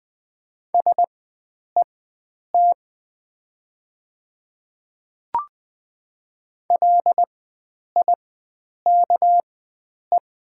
Morse code, heard as a keyed tone, about 21 wpm, spaced as for 9 wpm, 700 Hz. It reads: SET LIKE
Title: 4x Character Spacing - Top 500 Words - 20wpm